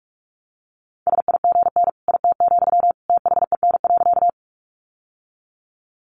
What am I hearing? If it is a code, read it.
Morse code, heard as a keyed tone, about 44 wpm, 700 Hz. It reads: SIGN IT, THEN.
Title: Sentences from Top 1000 Words 45wpm